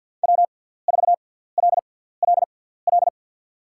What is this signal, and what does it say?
Morse code, heard as a keyed tone, about 48 wpm, spaced as for 17 wpm, 700 Hz. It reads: W4LLL
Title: USA Call Signs - 5 Characters - 6x Character Spacing - 50wpm